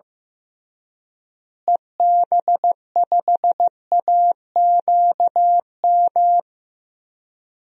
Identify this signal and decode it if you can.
Morse code, heard as a keyed tone, about 15 wpm, 700 Hz. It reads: EB5AQM
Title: DXCC-100 Call Signs - 6 Characters - Repeated 3x - 15wpm